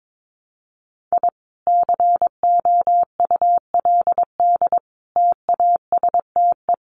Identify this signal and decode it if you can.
Morse code, heard as a keyed tone, about 22 wpm, 700 Hz. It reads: I COULD TASTE
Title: Sentences from Top 1200 Words 22wpm